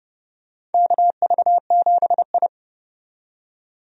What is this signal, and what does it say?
Morse code, heard as a keyed tone, about 30 wpm, 700 Hz. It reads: KV7I
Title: USA Call Signs - 4 Characters - 30wpm